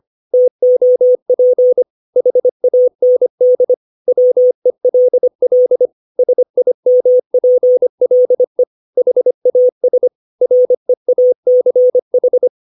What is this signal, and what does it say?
Morse code, heard as a keyed tone, about 25 wpm, 500 Hz. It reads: TOP HAND WELL SIMPLE HAS REACH
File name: 632 words / 632 Words 25WPM